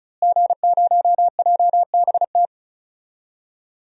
Morse code, heard as a keyed tone, about 34 wpm, 700 Hz. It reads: G0JBT
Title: DXCC-100 Call Signs - 5 Characters - Repeated 3x - 35wpm